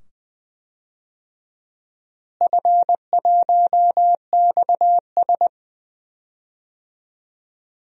Morse code, heard as a keyed tone, about 20 wpm, 700 Hz. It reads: F1XS
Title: DXCC-100 Call Signs - 4 Characters - Repeated 2x - 20wpm